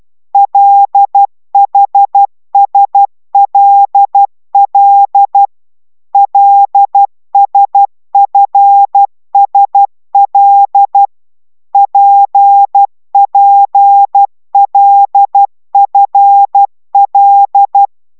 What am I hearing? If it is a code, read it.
Morse code, heard as a keyed tone, about 12 wpm, 800 Hz. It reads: LHSLL LSFSL PPLFL
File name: taviro tanulas / cw 69 10 10